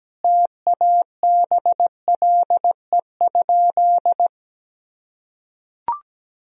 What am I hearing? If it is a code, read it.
Morse code, heard as a keyed tone, about 17 wpm, 700 Hz. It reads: TABLE?
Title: Sentences from Top 1000 Words - Review Part 3 - 17wpm